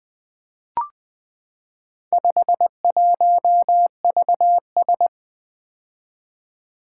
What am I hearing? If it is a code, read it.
Morse code, heard as a keyed tone, about 20 wpm, 700 Hz. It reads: 51VS
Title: Four Letter-Number Combinations - 20wpm